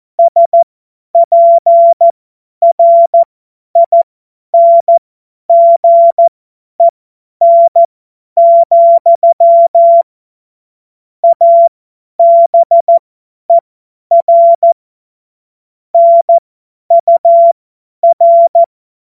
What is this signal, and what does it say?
Morse code, heard as a keyed tone, about 14 wpm, spaced as for 10 wpm, 680 Hz. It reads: SPRINGEN, ABER NUR
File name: Morsekurs / Witze / Witz 28.November 2023 50 F70